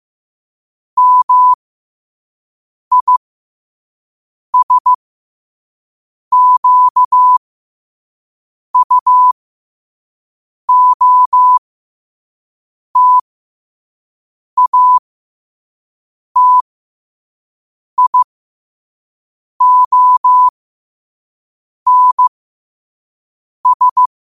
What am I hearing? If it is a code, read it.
Morse code, heard as a keyed tone, about 15 wpm, spaced as for 5 wpm, 1000 Hz. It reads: MISQUOTATIONS
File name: qotdfiles / 5wpm